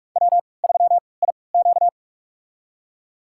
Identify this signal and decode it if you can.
Morse code, heard as a keyed tone, about 46 wpm, spaced as for 25 wpm, 700 Hz. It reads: W3IQ
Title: USA Call Signs - 4 Characters - 3x Character Spacing - 45wpm